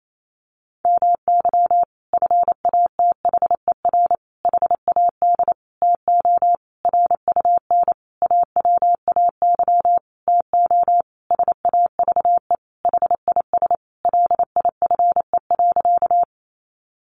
Morse code, heard as a keyed tone, about 28 wpm, 700 Hz. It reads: MY FATHER HAD TO RUN AWAY TO SAVE HIS LIFE.
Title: Sentences from Top 1100 Words 28wpm